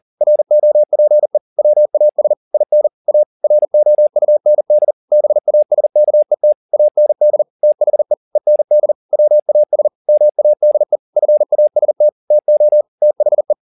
Morse code, heard as a keyed tone, about 40 wpm, 600 Hz. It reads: ROPE WAS IN A ROUND BASKET AND THE END WAS MADE FAST TO THE
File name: the old man and the sea 2 40wpm